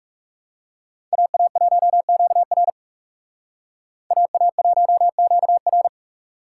Morse code, heard as a keyed tone, about 38 wpm, 700 Hz. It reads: AA1QR AA1QR